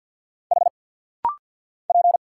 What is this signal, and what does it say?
Morse code, heard as a keyed tone, about 50 wpm, 700 Hz. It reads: H P
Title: Single Letter-Number Alt - Rapid-Fire 50wpm